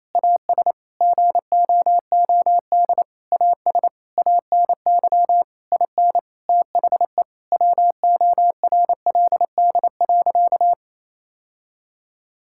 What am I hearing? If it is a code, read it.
Morse code, heard as a keyed tone, about 28 wpm, 700 Hz. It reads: AS GOOD AS ANY IN THE WORLD.
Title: Sentences from Top 1600 Words 28wpm